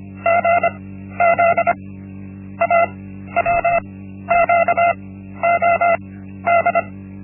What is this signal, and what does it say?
Morse code, heard as a keyed tone, about 25 wpm, 680 Hz. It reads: G Z A W Q O D
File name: Linksmakalnis-1